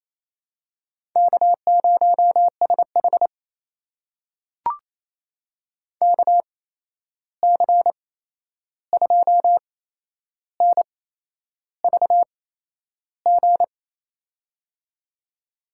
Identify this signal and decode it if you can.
Morse code, heard as a keyed tone, about 28 wpm, spaced as for 11 wpm, 700 Hz. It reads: K0SH KC2NVG
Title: USA Call Signs - 8x Character Spacing - 28wpm